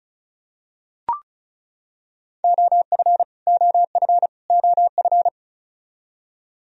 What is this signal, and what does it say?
Morse code, heard as a keyed tone, about 36 wpm, 700 Hz. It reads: OF OF OF
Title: Top 100 Words Repeated 35wpm